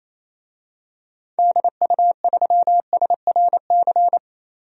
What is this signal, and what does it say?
Morse code, heard as a keyed tone, about 28 wpm, 700 Hz. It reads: DU3SRC
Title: DXCC-100 Call Signs - 6 Characters - 28wpm